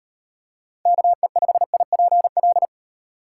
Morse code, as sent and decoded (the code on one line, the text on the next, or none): -.- . ..... .. .--. .-..
KE5IPL